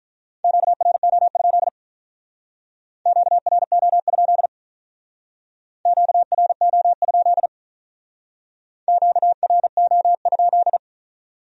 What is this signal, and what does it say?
Morse code, heard as a keyed tone, about 45 wpm, 700 Hz. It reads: QRO? QRO? QRO? QRO?